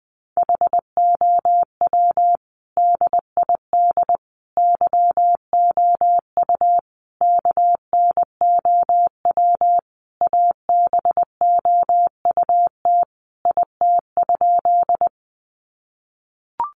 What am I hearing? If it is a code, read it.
Morse code, heard as a keyed tone, about 20 wpm, 700 Hz. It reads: HOW DID YOU KNOW ABOUT IT?
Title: Sentences from Top 100 Words 20wpm